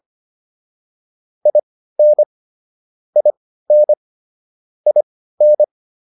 Morse code, heard as a keyed tone, about 25 wpm, spaced as for 15 wpm, 600 Hz. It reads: IN IN IN
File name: IN5 15